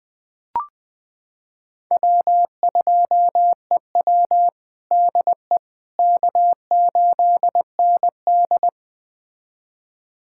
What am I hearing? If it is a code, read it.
Morse code, heard as a keyed tone, about 21 wpm, 700 Hz. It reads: W2EW DE K8ND
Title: USA Callsign DE Callsign - 4 Characters - 20wpm